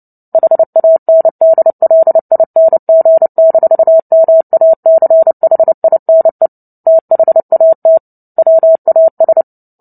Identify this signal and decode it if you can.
Morse code, heard as a keyed tone, about 30 wpm, 650 Hz. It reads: HANDLING-MACHINE THAT WAS
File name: Book Two / 03-Chapter 3 - 29 WPM